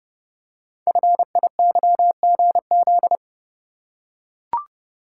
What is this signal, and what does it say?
Morse code, heard as a keyed tone, about 30 wpm, 700 Hz. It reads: FIYGZ
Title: Code Groups - 5 Letters - 30wpm